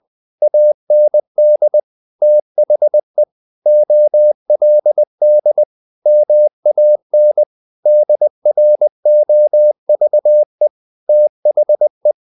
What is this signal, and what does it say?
Morse code, heard as a keyed tone, about 20 wpm, 600 Hz. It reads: AND THE OLD MAN DROVE THE
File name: the old man and the sea part 2 20wpm